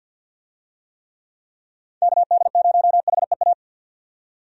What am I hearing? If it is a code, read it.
Morse code, heard as a keyed tone, about 51 wpm, 700 Hz. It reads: KD0HEA
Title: USA Call Signs - 6 Characters - 50wpm